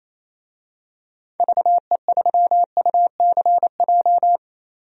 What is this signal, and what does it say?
Morse code, heard as a keyed tone, about 28 wpm, 700 Hz. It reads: VE3UCJ
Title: DXCC-100 Call Signs - 6 Characters - Repeated 3x - 28wpm